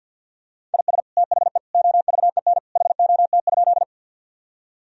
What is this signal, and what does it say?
Morse code, heard as a keyed tone, about 50 wpm, 700 Hz. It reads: IS THE OVEN HOT?
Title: Sentences from Top 1000 Words - Review Part 4 - 50wpm